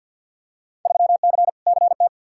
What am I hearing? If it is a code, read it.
Morse code, heard as a keyed tone, about 53 wpm, 700 Hz. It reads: 3C CT
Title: Field Day Exchanges Encore - 50wpm